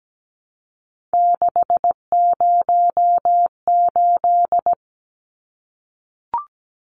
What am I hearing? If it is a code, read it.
Morse code, heard as a keyed tone, about 17 wpm, 700 Hz. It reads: 608
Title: Numbers - 3-Digit 17wpm